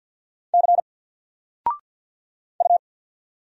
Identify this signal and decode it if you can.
Morse code, heard as a keyed tone, about 48 wpm, 700 Hz. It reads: C U
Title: Single Letter-Number 50wpm